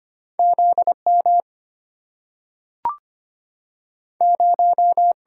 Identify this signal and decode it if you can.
Morse code, heard as a keyed tone, about 25 wpm, 700 Hz. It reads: ZM 0
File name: Two Letter-Number Combinations - 25wpm